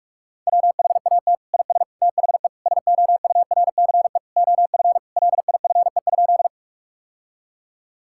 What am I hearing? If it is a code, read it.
Morse code, heard as a keyed tone, about 41 wpm, 700 Hz. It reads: WHAT IS THE SOURCE OF LIFE?
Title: Sentences from Top 1000 Words - Review Part 5 - 45wpm